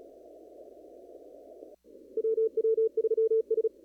Morse code, heard as a keyed tone, about 36 wpm, 440 Hz. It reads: WW3S
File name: CW / kt5j wpx cw-op-k5pi